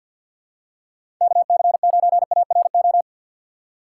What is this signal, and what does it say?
Morse code, heard as a keyed tone, about 53 wpm, 700 Hz. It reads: KC9ARO
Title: USA Call Signs - 6 Characters - 6x Character Spacing - 50wpm